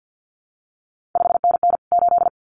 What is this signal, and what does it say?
Morse code, heard as a keyed tone, about 50 wpm, 700 Hz. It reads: 5NN 8
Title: CQ WW DX Contest Exchange - 50wpm